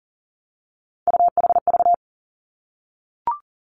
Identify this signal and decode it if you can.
Morse code, heard as a keyed tone, about 38 wpm, 690 Hz. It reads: UHV
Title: Three Challenging Letter-Number Combinations 40wpm